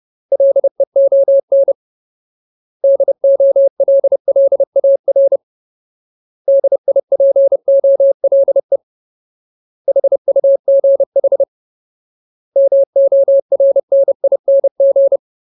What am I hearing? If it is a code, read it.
Morse code, heard as a keyed tone, about 30 wpm, 550 Hz. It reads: LEON DOLLAR DIPOLE HUGH MORNING
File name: Ultimate Word List / 30WPM Ultimate